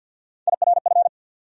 Morse code, heard as a keyed tone, about 49 wpm, 700 Hz. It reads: IRF
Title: USA Call Sign Postfixes - 3 Letters 50wpm